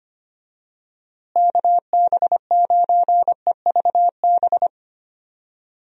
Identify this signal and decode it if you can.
Morse code, heard as a keyed tone, about 25 wpm, 700 Hz. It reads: KB9EVB
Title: USA Call Signs - 6 Characters - Repeated 3x - 25wpm